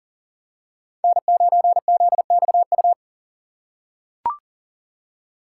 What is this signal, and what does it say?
Morse code, heard as a keyed tone, about 39 wpm, 700 Hz. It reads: N9ZXU
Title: USA Call Signs - 5 Characters - 2x Character Spacing - 40wpm